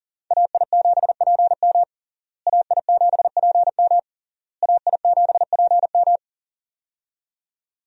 Morse code, heard as a keyed tone, about 40 wpm, 700 Hz. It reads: AI7PM AI7PM AI7PM
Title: USA Call Signs - 5 Characters - Repeated 3x - 40wpm